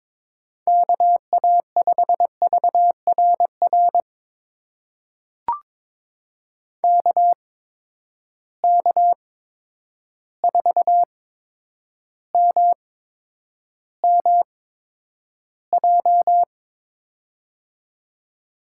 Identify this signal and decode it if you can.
Morse code, heard as a keyed tone, about 22 wpm, spaced as for 9 wpm, 700 Hz. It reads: KA5VRR KK4MMJ
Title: USA Call Signs - 6 Characters - 8x Character Spacing - 22wpm